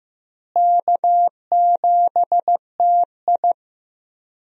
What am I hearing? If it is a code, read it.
Morse code, heard as a keyed tone, about 15 wpm, 700 Hz. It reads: K7TI
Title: USA Call Signs - 4 Characters - Repeated 3x - 15wpm